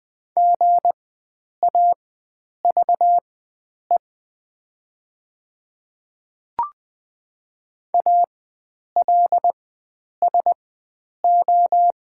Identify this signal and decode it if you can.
Morse code, heard as a keyed tone, about 20 wpm, spaced as for 9 wpm, 700 Hz. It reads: GAVE ALSO
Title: 4x Character Spacing - Top 500 Words - 20wpm